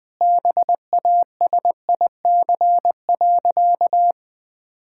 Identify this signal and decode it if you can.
Morse code, heard as a keyed tone, about 21 wpm, 700 Hz. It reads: BASIC.
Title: Sentences with 3-Word Phrases - Spoken Beginning - 20wpm